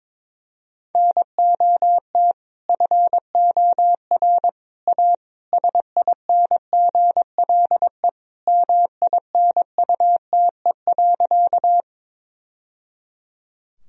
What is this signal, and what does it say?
Morse code, heard as a keyed tone, about 22 wpm, 700 Hz. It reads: NOT FOR A SINGLE MINUTE.
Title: Sentences from Top 1000 Words - Review Part 4 - 22wpm